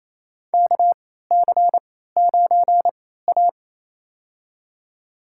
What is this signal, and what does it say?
Morse code, heard as a keyed tone, about 28 wpm, spaced as for 16 wpm, 700 Hz. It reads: KC9A
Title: USA Call Signs - 3x Character Spacing - 28wpm